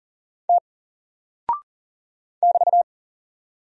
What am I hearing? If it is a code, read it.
Morse code, heard as a keyed tone, about 38 wpm, 700 Hz. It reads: T <BT>
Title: Single Character (Letter:Number:Punctuation:Prosign) 40wpm